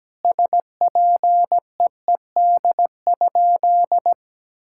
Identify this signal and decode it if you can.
Morse code, heard as a keyed tone, about 17 wpm, 700 Hz. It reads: SPEED?
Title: Sentences from Top 1000 Words - Review Part 5 - 17wpm